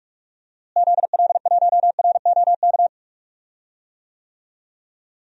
Morse code, heard as a keyed tone, about 41 wpm, 700 Hz. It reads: ZL1ROK
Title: DXCC-100 Call Signs - 6 Characters - Repeated 3x - 45wpm